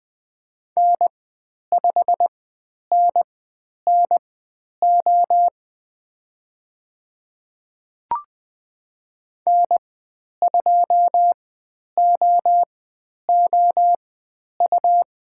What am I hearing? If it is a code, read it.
Morse code, heard as a keyed tone, about 20 wpm, spaced as for 10 wpm, 700 Hz. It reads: N5NNO N2OOU
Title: 109 Lesson 18g - Letter m - Call Signs 20wpm